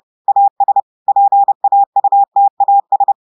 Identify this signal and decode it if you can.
Morse code, heard as a keyed tone, about 30 wpm, 800 Hz. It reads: AS PAUTAS
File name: portugais texte en clair 2 vitesse 30 mots minutes